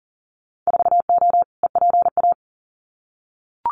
Morse code, heard as a keyed tone, about 38 wpm, 700 Hz. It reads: 4O EPA
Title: Winter Field Day Exchanges 40wpm